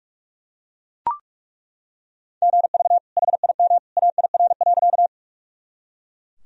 Morse code, heard as a keyed tone, about 43 wpm, 700 Hz. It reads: GV HIM AIR.
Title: Sentences from Top 200 Words and Abbreviations 45wpm